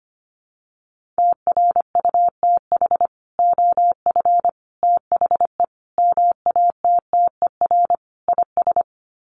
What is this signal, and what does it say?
Morse code, heard as a keyed tone, about 25 wpm, 700 Hz. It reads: TRUTH OF THE MATTER IS
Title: Sets of 5 Words Encore2 25wpm